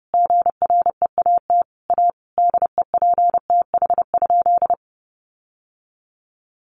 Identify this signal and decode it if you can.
Morse code, heard as a keyed tone, about 30 wpm, 700 Hz. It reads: GREAT A DEPTH?
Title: Sentences from Top 1200 Words 30wpm